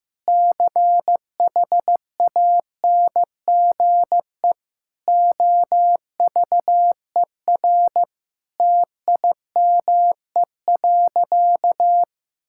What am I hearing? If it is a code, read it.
Morse code, heard as a keyed tone, about 15 wpm, 700 Hz. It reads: CHANGE OVER TIME.
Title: Sentences from Top 1000 Words - Review Part 6 - 15wpm